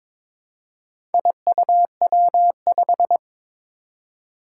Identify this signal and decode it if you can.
Morse code, heard as a keyed tone, about 22 wpm, 700 Hz. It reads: IUW5
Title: Four Challenging Letter-Number Combinations 22wpm